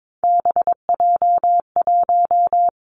Morse code, heard as a keyed tone, about 22 wpm, 700 Hz. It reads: BJ1
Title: Three Challenging Letter-Number Combinations 22wpm